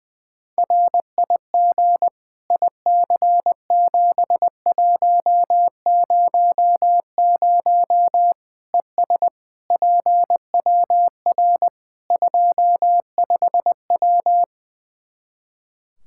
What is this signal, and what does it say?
Morse code, heard as a keyed tone, about 21 wpm, 700 Hz. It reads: RIG IC7100 ES PWR 25W
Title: K1USN SST QSO Practice 20wpm